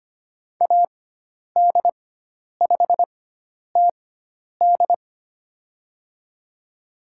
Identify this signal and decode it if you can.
Morse code, heard as a keyed tone, about 25 wpm, spaced as for 10 wpm, 700 Hz. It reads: AD5TD
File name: Lesson 16g - Letter c - Call Signs 25wpm